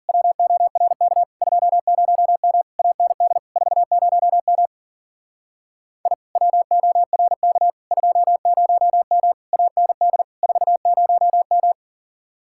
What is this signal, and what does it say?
Morse code, heard as a keyed tone, about 50 wpm, 700 Hz. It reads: WORK 20M AND 40M I WORK 20M AND 40M
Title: Speed-Racing Common QSO Phrases - 40wpm